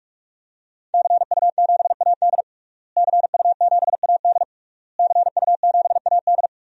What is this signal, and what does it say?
Morse code, heard as a keyed tone, about 46 wpm, 700 Hz. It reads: CU7AD CU7AD CU7AD
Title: DXCC-100 Call Signs - 5 Characters - Repeated 3x - 45wpm